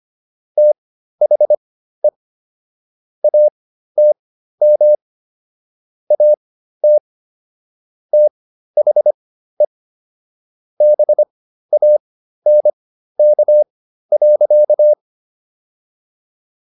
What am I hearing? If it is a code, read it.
Morse code, heard as a keyed tone, about 25 wpm, spaced as for 13 wpm, 600 Hz. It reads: THE ATM AT THE BANK.